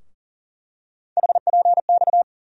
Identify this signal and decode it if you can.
Morse code, heard as a keyed tone, about 39 wpm, 700 Hz. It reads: HPX
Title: USA Call Sign Postfixes - 3 Letters 40wpm